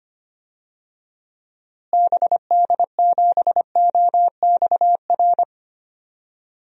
Morse code, heard as a keyed tone, about 25 wpm, 700 Hz. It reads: BD7OXR
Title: DXCC-100 Call Signs - 6 Characters - Repeated 2x - 25wpm